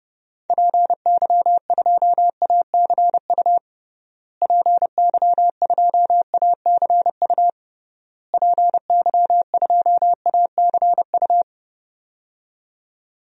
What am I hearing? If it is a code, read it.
Morse code, heard as a keyed tone, about 30 wpm, 700 Hz. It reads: PY2ACU PY2ACU PY2ACU
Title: DXCC-100 Call Signs - 6 Characters - Repeated 3x - 30wpm